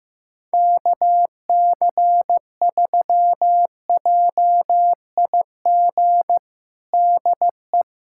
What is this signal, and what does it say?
Morse code, heard as a keyed tone, about 15 wpm, 700 Hz. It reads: KC3JIG DE
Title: USA Callsign DE Callsign 15wpm